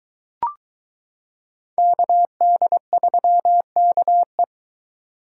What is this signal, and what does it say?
Morse code, heard as a keyed tone, about 23 wpm, 700 Hz. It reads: KD3KE